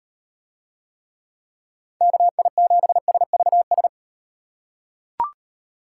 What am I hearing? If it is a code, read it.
Morse code, heard as a keyed tone, about 36 wpm, 700 Hz. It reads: KI7SVS